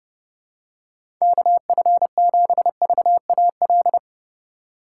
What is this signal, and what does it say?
Morse code, heard as a keyed tone, about 30 wpm, 700 Hz. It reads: KF7VAL
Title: USA Call Signs - 6 Characters - 3x Character Spacing - 30wpm